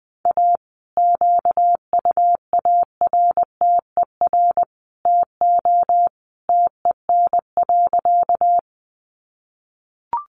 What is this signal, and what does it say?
Morse code, heard as a keyed tone, about 20 wpm, 700 Hz. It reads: A QUARTER TO TEN.
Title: Sentences from Top 1300 Words 20wpm